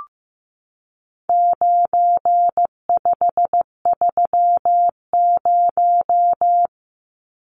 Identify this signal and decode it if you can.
Morse code, heard as a keyed tone, about 15 wpm, 700 Hz. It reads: 9530
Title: Numbers - 4-Digit 15wpm